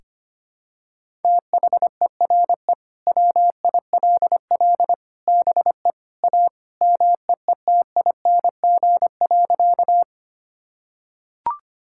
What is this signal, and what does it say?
Morse code, heard as a keyed tone, about 25 wpm, 700 Hz. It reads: THERE WILL BE A MEETING.
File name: practice / 25wpm / Sentences with 3-Word Phrases - 25wpm